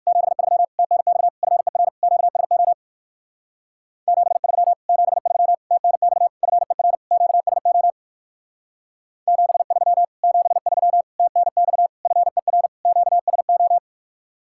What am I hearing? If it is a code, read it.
Morse code, heard as a keyed tone, about 55 wpm, 700 Hz. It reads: 73 TNX FER QSO 73 73 TNX FER QSO 73 73 TNX FER QSO
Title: Speed-Racing Common QSO Phrases - 45wpm